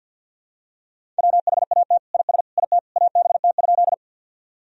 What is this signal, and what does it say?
Morse code, heard as a keyed tone, about 47 wpm, 700 Hz. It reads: WHAT IS IT ABT?
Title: Sentences from Top 100 Words and Abbreviations 50wpm